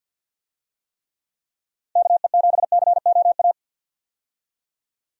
Morse code, heard as a keyed tone, about 51 wpm, 700 Hz. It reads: KE7COA
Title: USA Call Signs - 6 Characters - 50wpm